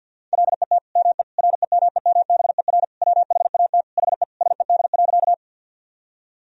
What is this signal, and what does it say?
Morse code, heard as a keyed tone, about 50 wpm, 700 Hz. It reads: LET ME REMEMBER WHAT HE SED.
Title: Sentences from Top 400 Words and Abbreviations 50wpm